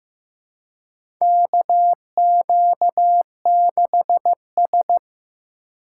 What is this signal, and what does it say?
Morse code, heard as a keyed tone, about 15 wpm, 700 Hz. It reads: KQ6S
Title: USA Call Signs - 4 Characters - 4x Character Spacing - 15wpm